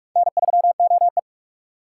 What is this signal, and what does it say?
Morse code, heard as a keyed tone, about 43 wpm, 700 Hz. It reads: N3OE